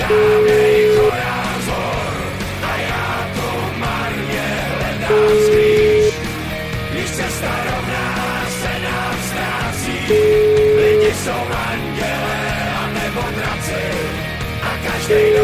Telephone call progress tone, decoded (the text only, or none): ringback tone